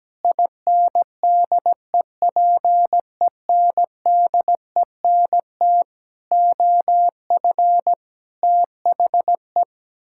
Morse code, heard as a keyed tone, about 17 wpm, 700 Hz. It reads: INDEPENDENT OF THE
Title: Sentences from Top 1000 Words - Review Part 6 - 17wpm